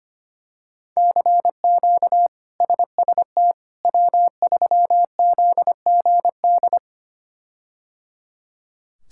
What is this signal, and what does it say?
Morse code, heard as a keyed tone, about 25 wpm, 700 Hz. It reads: CQ SST W3ZGD
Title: K1USN SST QSO Practice - 25wpm